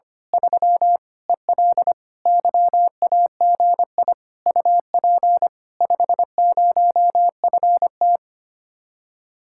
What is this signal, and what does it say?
Morse code, heard as a keyed tone, about 24 wpm, 700 Hz. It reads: 3 EL YAGI UP 50FT
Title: Speed-Racing Common QSO Phrases Encore - 25wpm